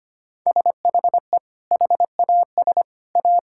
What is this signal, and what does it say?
Morse code, heard as a keyed tone, about 25 wpm, 700 Hz. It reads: SHE HAS A
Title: Sentences from Top 1600 Words 25wpm